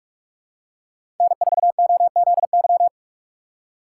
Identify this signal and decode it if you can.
Morse code, heard as a keyed tone, about 46 wpm, 700 Hz. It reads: N4OZY
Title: USA Call Signs - 5 Characters - 2x Character Spacing - 45wpm